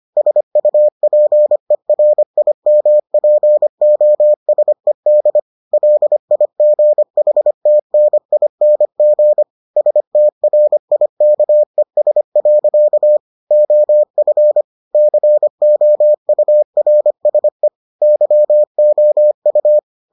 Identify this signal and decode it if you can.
Morse code, heard as a keyed tone, about 25 wpm, 600 Hz. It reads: SUPERIMPOSED LIGHTNING STRIKES. OF COURSE YOU